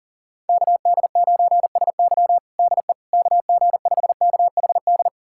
Morse code, heard as a keyed tone, about 40 wpm, 700 Hz. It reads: KD9SY DE KG5KHD
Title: USA Callsign DE Callsign 40wpm